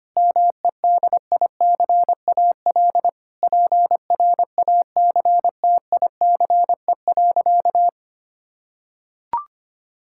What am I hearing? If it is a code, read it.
Morse code, heard as a keyed tone, about 25 wpm, 700 Hz. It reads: MEDICAL PRACTICE.
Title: Sentences from Top 1000 Words - Review Part 5 - 25wpm